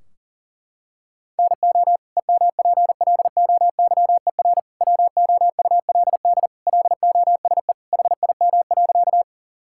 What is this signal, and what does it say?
Morse code, heard as a keyed tone, about 40 wpm, 700 Hz. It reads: NO EMPLOYER WOULD LOSE HIM.
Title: Sentences from Top 1600 Words 40wpm